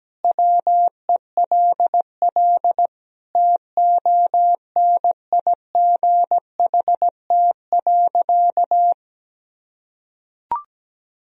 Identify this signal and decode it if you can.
Morse code, heard as a keyed tone, about 17 wpm, 700 Hz. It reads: WELL TONIGHT.
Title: Sentences from Top 1000 Words - Review Part 3 - 17wpm